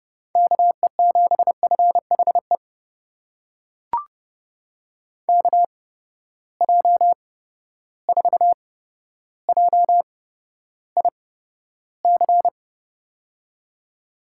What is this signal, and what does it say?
Morse code, heard as a keyed tone, about 32 wpm, spaced as for 13 wpm, 700 Hz. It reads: KE7FHE KJ4JIC
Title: USA Call Signs - 6 Characters - 8x Character Spacing - 30wpm